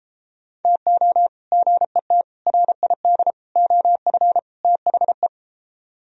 Morse code, heard as a keyed tone, about 33 wpm, 700 Hz. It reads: TO GET RID OF THE